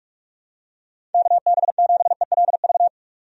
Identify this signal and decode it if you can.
Morse code, heard as a keyed tone, about 46 wpm, 700 Hz. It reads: KB7ELV
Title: USA Call Signs - 6 Characters - 45wpm